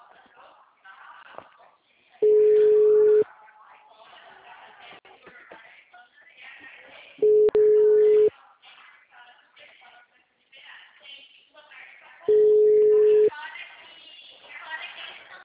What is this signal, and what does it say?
Ringback tone (Europe, 425 Hz), with speech.
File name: out- -2038- - - .405